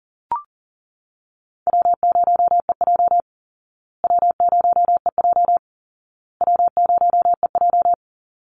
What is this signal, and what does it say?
Morse code, heard as a keyed tone, about 37 wpm, 700 Hz. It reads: W0EJ W0EJ W0EJ
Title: Easy Call Signs Repeated 3x - 40wpm